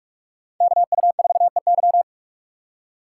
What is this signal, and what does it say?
Morse code, heard as a keyed tone, about 58 wpm, 700 Hz. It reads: KU4EY